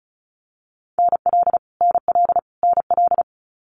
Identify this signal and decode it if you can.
Morse code, heard as a keyed tone, about 34 wpm, 700 Hz. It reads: NL NL NL
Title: Canadian Provinces and Territories - Repeated 3x - 35wpm